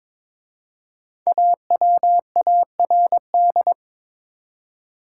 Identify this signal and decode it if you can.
Morse code, heard as a keyed tone, about 22 wpm, 700 Hz. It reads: AWARD
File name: 5 Letter Words - First 3 Letters Given - 22wpm